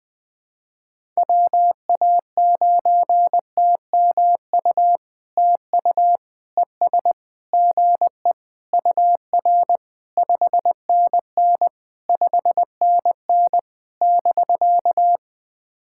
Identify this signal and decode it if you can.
Morse code, heard as a keyed tone, about 21 wpm, 700 Hz. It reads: WA9TMU TU ES GE UR 5NN 5NN <BK>
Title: POTA QSO Practice - 20wpm